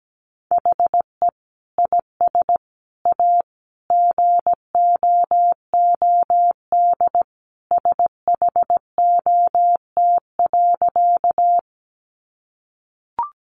Morse code, heard as a keyed tone, about 17 wpm, 700 Hz. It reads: HE IS A GOOD SHOT.
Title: Sentences from Top 1100 Words 17wpm